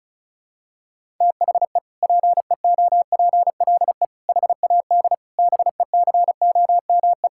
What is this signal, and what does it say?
Morse code, heard as a keyed tone, about 34 wpm, 700 Hz. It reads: THE PEOPLE HAD BECOME
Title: Sentences from Top 1000 Words - Review Part 6 - 35wpm